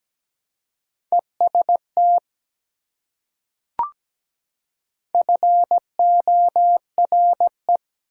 Morse code, heard as a keyed tone, about 17 wpm, 700 Hz. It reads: EST FORE
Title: Prefixes and Suffixes 17wpm